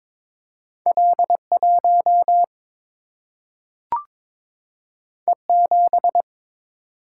Morse code, heard as a keyed tone, about 22 wpm, 700 Hz. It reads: L 1 E 7
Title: Single-Letter Single-Number 22wpm